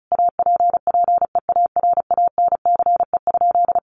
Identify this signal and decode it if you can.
Morse code, heard as a keyed tone, about 34 wpm, 700 Hz. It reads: APPEARANCE?
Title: Sentences from Top 1100 Words 35wpm